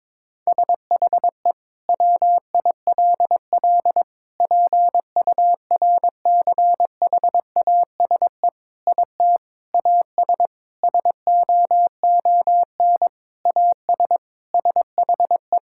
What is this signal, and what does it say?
Morse code, heard as a keyed tone, about 22 wpm, 700 Hz. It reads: SHE WILL PURCHASE IT AS SOON AS SHE
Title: Sentences from Top 1000 Words - Review Part 5 - 22wpm